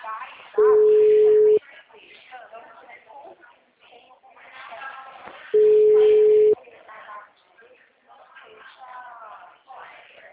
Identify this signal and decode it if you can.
Ringback tone (Europe, 425 Hz), with speech.